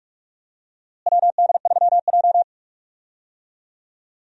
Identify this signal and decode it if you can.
Morse code, heard as a keyed tone, about 47 wpm, 700 Hz. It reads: WD3J
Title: USA Call Signs - 4 Characters - Repeated 2x - 45wpm